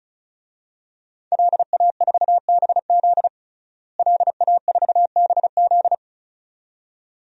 Morse code, heard as a keyed tone, about 35 wpm, 700 Hz. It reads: LA4BZ LA4BZ
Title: DXCC-100 Call Signs - 5 Characters - Repeated 2x - 35wpm